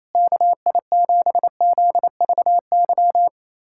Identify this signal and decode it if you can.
Morse code, heard as a keyed tone, about 27 wpm, 700 Hz. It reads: KI7ZVY
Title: USA Callsign DE Callsign - 6 Characters - 28wpm